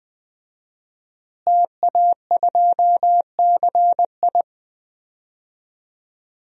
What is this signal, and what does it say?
Morse code, heard as a keyed tone, about 21 wpm, 700 Hz. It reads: TA2CI
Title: DXCC-100 Call Signs - 5 Characters - Repeated 2x - 20wpm